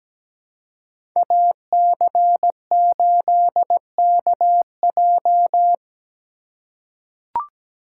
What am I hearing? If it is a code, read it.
Morse code, heard as a keyed tone, about 17 wpm, 700 Hz. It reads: AC8KJ
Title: USA Call Signs - 2x Character Spacing - 17wpm